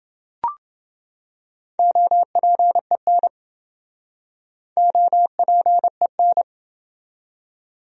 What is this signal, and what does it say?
Morse code, heard as a keyed tone, about 28 wpm, 700 Hz. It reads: OPEN OPEN